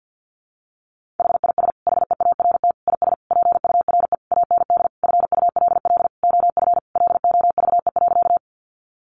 Morse code, heard as a keyed tone, about 50 wpm, 700 Hz. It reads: HIS HEART IS PURE AND FULL OF LOVE.
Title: Sentences from Top 1500 Words 50wpm